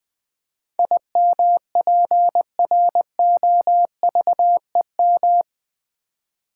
Morse code, heard as a keyed tone, about 20 wpm, 700 Hz. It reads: IMPROVEM
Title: Build A Word - 4 Letters At A Time - 11-Letter Words - 20wpm